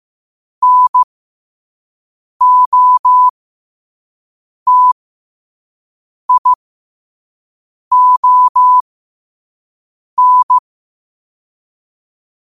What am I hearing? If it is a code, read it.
Morse code, heard as a keyed tone, about 15 wpm, spaced as for 5 wpm, 1000 Hz. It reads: NOTION
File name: qotdfiles / 5wpm